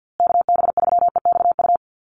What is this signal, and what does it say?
Morse code, heard as a keyed tone, about 50 wpm, 700 Hz. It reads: KB3EXU
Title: USA Call Signs Repeated 2x 50wpm